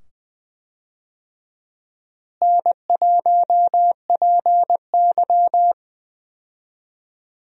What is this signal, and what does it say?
Morse code, heard as a keyed tone, about 20 wpm, 700 Hz. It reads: N1PY
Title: USA Call Signs - 4 Characters - Repeated 2x - 20wpm